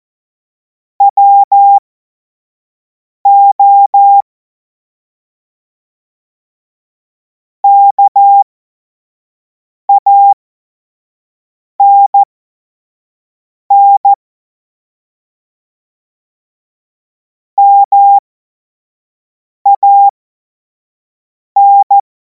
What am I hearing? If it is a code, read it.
Morse code, heard as a keyed tone, about 14 wpm, spaced as for 5 wpm, 800 Hz. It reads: WO KANN MAN
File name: Morsekurs / Witze / Witz 28. April 2026 25 F70